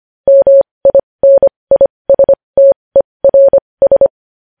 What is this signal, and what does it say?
Morse code, heard as a keyed tone, about 25 wpm, spaced as for 20 wpm, 550 Hz. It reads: MINISTERS
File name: podcast / News headlines-1x-25x20